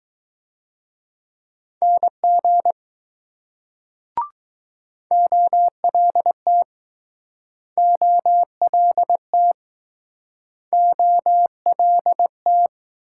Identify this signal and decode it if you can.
Morse code, heard as a keyed tone, about 21 wpm, 700 Hz. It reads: NG OLT OLT OLT